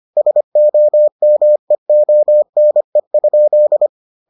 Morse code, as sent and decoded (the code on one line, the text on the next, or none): ... --- -- . --- -. ..--..
SOMEONE?